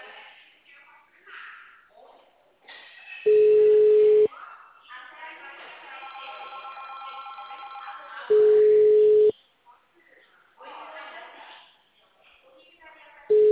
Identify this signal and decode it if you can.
Ringback tone (Europe, 425 Hz), with speech.